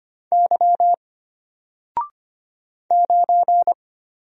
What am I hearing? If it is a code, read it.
Morse code, heard as a keyed tone, about 25 wpm, 700 Hz. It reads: Y 9
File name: Single-Letter-Number-25wpm